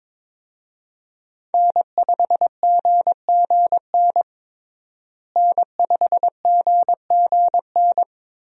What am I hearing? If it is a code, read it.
Morse code, heard as a keyed tone, about 22 wpm, 700 Hz. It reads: N5GGN N5GGN
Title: USA Call Signs - 5 Characters - Repeated 3x - 22wpm